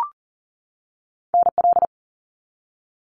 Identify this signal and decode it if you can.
Morse code, heard as a keyed tone, about 41 wpm, 700 Hz. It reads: NL
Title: Canadian Provinces and Territories 40wpm